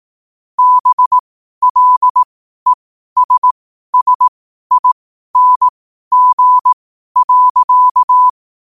Morse code, heard as a keyed tone, about 18 wpm, spaced as for 12 wpm, 1000 Hz. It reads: BLESSING.
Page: New quotes every day in morse code at 12 Words per minute.